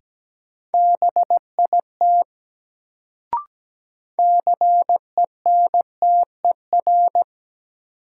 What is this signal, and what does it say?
Morse code, heard as a keyed tone, about 17 wpm, 700 Hz. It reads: BIT CENTER
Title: Sets of 3 Words - New Word Drill - 17wpm